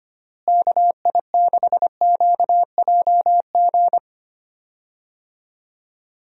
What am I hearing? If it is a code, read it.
Morse code, heard as a keyed tone, about 25 wpm, 700 Hz. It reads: KI6QJG
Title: USA Call Signs - 6 Characters - Repeated 2x - 25wpm